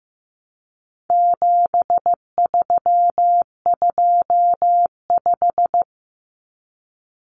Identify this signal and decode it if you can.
Morse code, heard as a keyed tone, about 15 wpm, 700 Hz. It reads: 7325
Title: Numbers - 4-Digit 15wpm